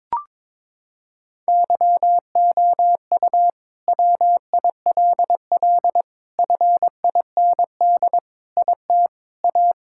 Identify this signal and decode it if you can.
Morse code, heard as a keyed tone, about 22 wpm, 700 Hz. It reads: YOU WILL FIND IT A
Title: Sentences from Top 1600 Words 22wpm